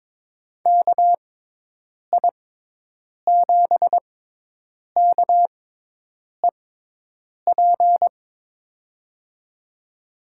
Morse code, heard as a keyed tone, about 22 wpm, spaced as for 8 wpm, 700 Hz. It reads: KI7KEP
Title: USA Call Signs - 6 Characters - 6x Character Spacing - 22wpm